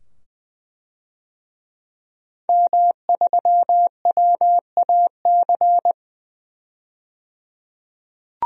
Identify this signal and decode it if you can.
Morse code, heard as a keyed tone, about 21 wpm, 700 Hz. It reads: M3WAC
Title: DXCC-100 Call Signs - 5 Characters - 20wpm